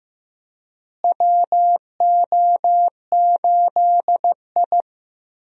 Morse code, heard as a keyed tone, about 15 wpm, 700 Hz. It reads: WO8I
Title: USA Call Signs - 4 Characters - Repeated 3x - 15wpm